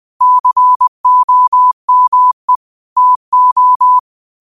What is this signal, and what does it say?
Morse code, heard as a keyed tone, about 20 wpm, 1000 Hz. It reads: COME TO